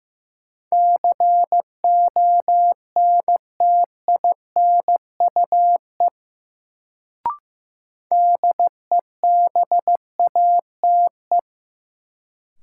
Morse code, heard as a keyed tone, about 15 wpm, 700 Hz. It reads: CONTINUE DEBATE
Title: Sentences from Top 1100 Words - New Word Drill Alt - 15wpm